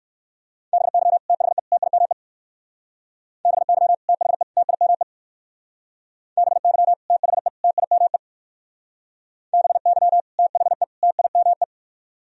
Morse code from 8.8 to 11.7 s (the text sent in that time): BY THE TIME